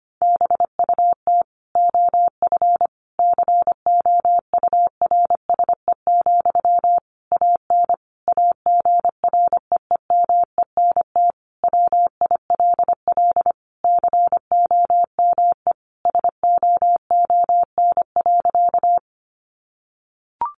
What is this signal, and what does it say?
Morse code, heard as a keyed tone, about 25 wpm, 700 Hz. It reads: BUT OF COURSE, AN AGREEMENT WILL COME SOON.
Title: Sentences from Top 1500 Words 25wpm